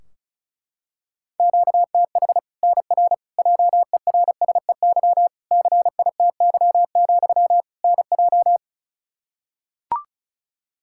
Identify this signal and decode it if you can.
Morse code, heard as a keyed tone, about 36 wpm, 700 Hz. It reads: QTH NR JERSEY CITY, NJ
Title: K1USN SST QSO Practice 35wpm